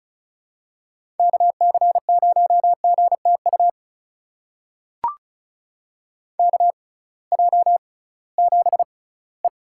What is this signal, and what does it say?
Morse code, heard as a keyed tone, about 34 wpm, spaced as for 20 wpm, 700 Hz. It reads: KC0GTU KJ7E
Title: USA Call Signs - 6 Characters - 6x Character Spacing - 35wpm